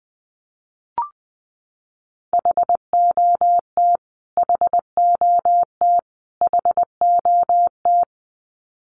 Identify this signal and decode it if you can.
Morse code, heard as a keyed tone, about 20 wpm, 700 Hz. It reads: HOT HOT HOT
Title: Top 500 Words Repeated 20wpm